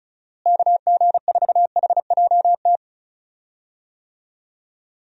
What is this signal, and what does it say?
Morse code, heard as a keyed tone, about 34 wpm, 700 Hz. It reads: KG4HJT
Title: USA Call Signs - 6 Characters - Repeated 3x - 35wpm